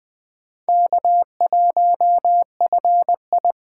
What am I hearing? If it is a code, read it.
Morse code, heard as a keyed tone, about 20 wpm, 700 Hz. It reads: K1FI
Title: Speed Racing - USA 1x2 Call Signs 17wpm